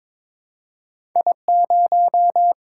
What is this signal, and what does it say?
Morse code, heard as a keyed tone, about 22 wpm, 700 Hz. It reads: I0
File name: practice / 22wpm / Two Letter-Number Combinations - 22wpm